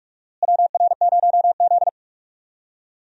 Morse code, heard as a keyed tone, about 46 wpm, 700 Hz. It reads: WR0Z
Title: USA Call Signs - 4 Characters - Repeated 2x - 45wpm